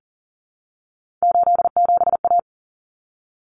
Morse code, heard as a keyed tone, about 40 wpm, 700 Hz. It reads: 87A
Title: Common Cut Numbers - 3-Digit 40wpm